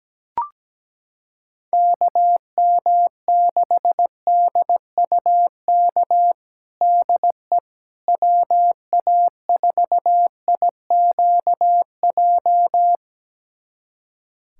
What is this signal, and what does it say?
Morse code, heard as a keyed tone, about 17 wpm, 700 Hz. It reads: KM6DUK DE WA4IQJ
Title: USA Callsign DE Callsign - 6 Characters - 17wpm